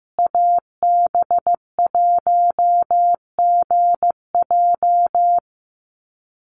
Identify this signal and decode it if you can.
Morse code, heard as a keyed tone, about 15 wpm, 700 Hz. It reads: AB1GJ
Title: Call Signs Repeated 3x - 15wpm